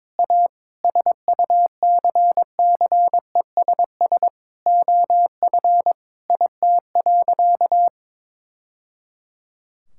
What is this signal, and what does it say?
Morse code, heard as a keyed tone, about 22 wpm, 700 Hz. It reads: A SUCCESS OF IT.
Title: Sentences from Top 1000 Words - Review Part 6 - 22wpm